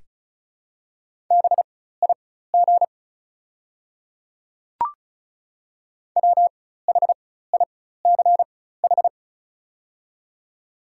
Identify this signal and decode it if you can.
Morse code, heard as a keyed tone, about 36 wpm, spaced as for 16 wpm, 700 Hz. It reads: BIG WHICH
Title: 4x Character Spacing - Top 200 Words - 35wpm